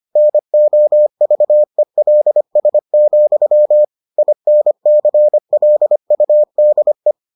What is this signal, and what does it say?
Morse code, heard as a keyed tone, about 25 wpm, 600 Hz. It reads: NOVELS, INCLUDE